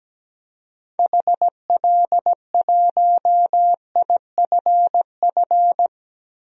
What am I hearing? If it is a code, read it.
Morse code, heard as a keyed tone, about 17 wpm, 700 Hz. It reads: HL1IFF
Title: DXCC-100 Call Signs - 6 Characters - 17wpm